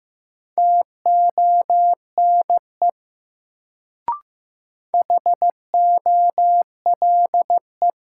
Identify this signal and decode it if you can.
Morse code, heard as a keyed tone, about 15 wpm, 700 Hz. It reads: TONE HOLE
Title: Sentences from Top 1600 Words - New Word Drill Alt - 15wpm